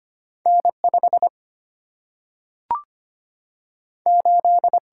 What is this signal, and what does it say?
Morse code, heard as a keyed tone, about 25 wpm, 700 Hz. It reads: N 5 8
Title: Single-Letter Single-Number 25wpm